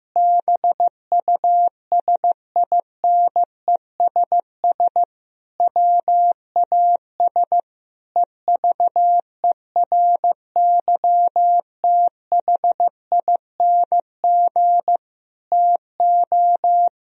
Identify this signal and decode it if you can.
Morse code, heard as a keyed tone, about 15 wpm, 700 Hz. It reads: BUSINESS WAS EVERYTHING TO
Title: Sentences from Top 1000 Words - Review Part 3 - 15wpm